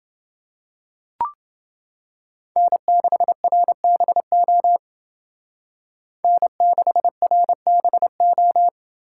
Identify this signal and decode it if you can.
Morse code, heard as a keyed tone, about 28 wpm, 700 Hz. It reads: N6RBO N6RBO